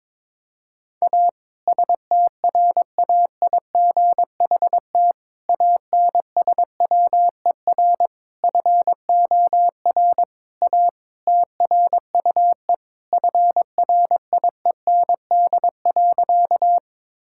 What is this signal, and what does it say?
Morse code, heard as a keyed tone, about 21 wpm, 700 Hz. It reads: A STRAIGHT ANSWER FOR A TRUE FRIEND.
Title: Sentences from Top 1000 Words - Review Part 6 - 22wpm